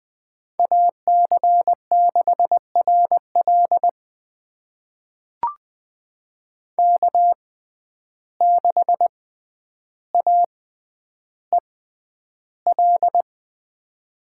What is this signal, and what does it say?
Morse code, heard as a keyed tone, about 21 wpm, spaced as for 10 wpm, 700 Hz. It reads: AC6RL K6AEL
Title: USA Call Signs - 5 Characters - 6x Character Spacing - 20wpm